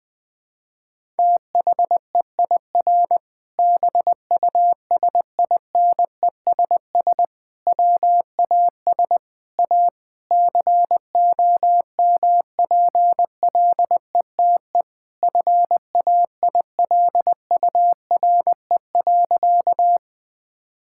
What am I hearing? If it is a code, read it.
Morse code, heard as a keyed tone, about 20 wpm, 700 Hz. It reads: THEIR BUSINESS WAS A COMPLETE FAILURE.
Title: Sentences from Top 1000 Words - Review Part 6 - 20wpm